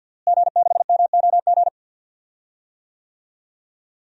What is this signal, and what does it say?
Morse code, heard as a keyed tone, about 52 wpm, 700 Hz. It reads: G6MOG
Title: DXCC-100 Call Signs - 5 Characters - Repeated 3x - 50wpm